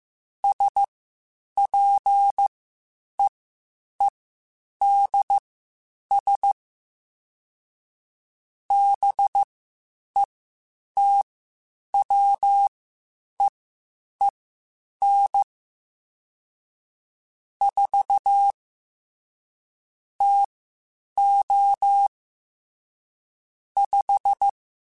Morse code, heard as a keyed tone, about 15 wpm, spaced as for 8 wpm, 800 Hz. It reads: SPEEDS BETWEEN 4 TO 5
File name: Spaced Characters / Spaced Characters-mp3s / Weather 4 21 sp @13